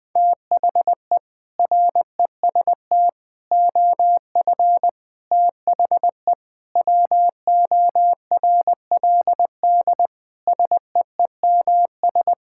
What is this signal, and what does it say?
Morse code, heard as a keyed tone, about 20 wpm, 700 Hz. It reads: THE REST OF THE WORLD SEEMS
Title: Sentences with 5-Word Phrases - 20wpm